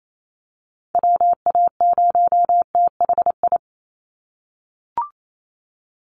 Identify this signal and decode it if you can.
Morse code, heard as a keyed tone, about 28 wpm, 700 Hz. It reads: WA0THI
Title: USA Call Signs 28wpm